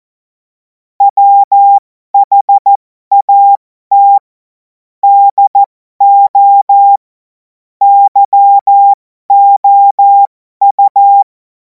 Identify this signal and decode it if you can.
Morse code, heard as a keyed tone, about 14 wpm, spaced as for 12 wpm, 800 Hz. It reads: WHAT DO YOU
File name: Morsekurs / Witze